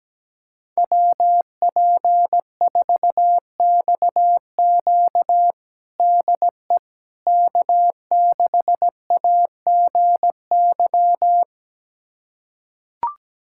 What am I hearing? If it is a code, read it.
Morse code, heard as a keyed tone, about 17 wpm, 700 Hz. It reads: WP4XQ DE K6AGY
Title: USA Callsign DE Callsign 17wpm